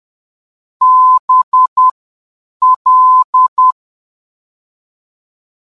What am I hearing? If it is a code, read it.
Morse code, heard as a keyed tone, about 10 wpm, spaced as for 7 wpm, 1020 Hz. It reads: BL
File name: mp-BL NDB